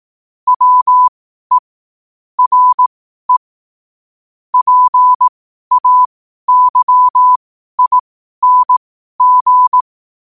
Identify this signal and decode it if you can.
Morse code, heard as a keyed tone, about 18 wpm, spaced as for 12 wpm, 1000 Hz. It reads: WE RE PAYING
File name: qotdfiles / 12wpm